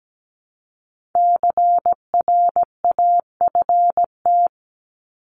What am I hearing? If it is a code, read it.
Morse code, heard as a keyed tone, about 17 wpm, 700 Hz. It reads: CRAFT
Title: Top 751-1000 Words 17wpm